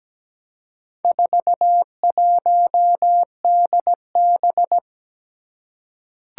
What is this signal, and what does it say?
Morse code, heard as a keyed tone, about 17 wpm, 700 Hz. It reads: 41DB
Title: Four Challenging Letter-Number Combinations 17wpm